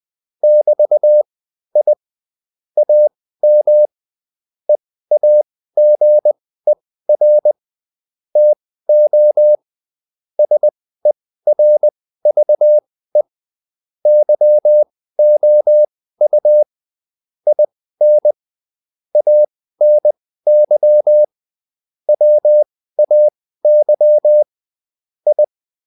Morse code, heard as a keyed tone, about 20 wpm, spaced as for 14 wpm, 600 Hz. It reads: <BT> I AM EAGER TO SERVE YOU IN ANY WAY I